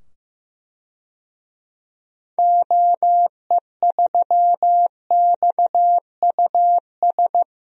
Morse code, heard as a keyed tone, about 15 wpm, 700 Hz. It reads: OE3XUS
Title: DXCC-100 Call Signs - 6 Characters - 15wpm